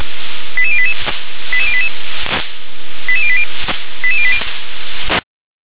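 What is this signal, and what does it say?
Five-tone selective call (ZVEI), heard as quick runs of five tones, tones of about 70 ms. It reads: 88088 88088 88080 88080
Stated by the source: ZVEI five-tone call 88152 received by SDR.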